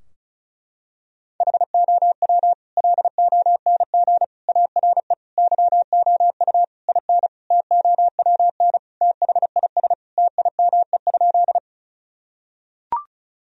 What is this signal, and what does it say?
Morse code, heard as a keyed tone, about 36 wpm, 700 Hz. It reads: HOW LONG ARE YOU IN TOWN THIS TIME?
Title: Sentences from Top 1000 Words - Review Part 2 - 35wpm